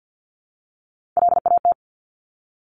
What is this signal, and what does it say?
Morse code, heard as a keyed tone, about 50 wpm, 700 Hz. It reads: LAT
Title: Top Three Letter Combinations in Words 50wpm